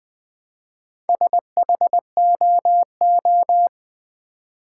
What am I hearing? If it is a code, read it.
Morse code, heard as a keyed tone, about 20 wpm, 700 Hz. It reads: SHOO
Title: Build A Word - 1 Letter At A Time - 3 to 5-Letter Words - 20wpm